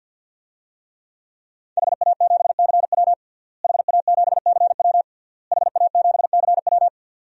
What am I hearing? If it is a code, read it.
Morse code, heard as a keyed tone, about 50 wpm, 700 Hz. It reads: HA7CW HA7CW HA7CW
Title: DXCC-100 Call Signs - 5 Characters - Repeated 3x - 50wpm